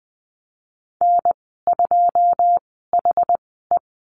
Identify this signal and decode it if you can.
Morse code, heard as a keyed tone, about 20 wpm, spaced as for 15 wpm, 700 Hz. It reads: N2HE
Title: 147 Lesson 24g - Letter y - Call Signs 20wpm